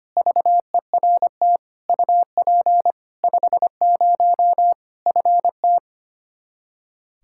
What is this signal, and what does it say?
Morse code, heard as a keyed tone, about 25 wpm, 700 Hz. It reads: VERT UP 50 FT
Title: Practice QSO - Exchange 2 - Info - 25wpm